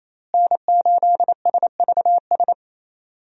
Morse code, heard as a keyed tone, about 28 wpm, 700 Hz. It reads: N8SVS
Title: USA Call Signs - 5 Characters - Repeated 2x - 28wpm